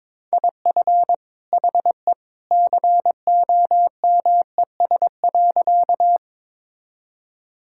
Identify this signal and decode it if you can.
Morse code, heard as a keyed tone, about 22 wpm, 700 Hz. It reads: IF HE COMES.
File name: practice / 22wpm / Sentences with 2-Word Phrases - 22wpm